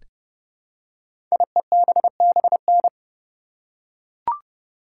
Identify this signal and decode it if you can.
Morse code, heard as a keyed tone, about 30 wpm, 690 Hz. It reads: IEBBN
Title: Code Groups - 5 Letters - 30wpm